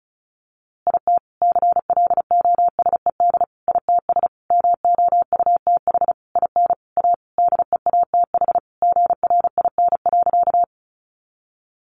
Morse code, heard as a keyed tone, about 34 wpm, 700 Hz. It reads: IT CLOSED ITS MOUTH IN A DEATH GRIN.
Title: A Sound Of Thunder 35wpm